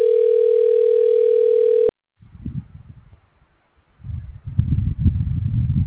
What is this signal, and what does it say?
Ringback tone (North America), 440 + 480 Hz.